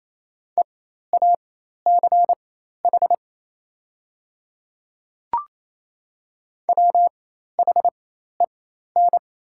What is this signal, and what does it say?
Morse code, heard as a keyed tone, about 28 wpm, spaced as for 13 wpm, 700 Hz. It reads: EACH WHEN
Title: 4x Character Spacing - Top 100 Words - 28wpm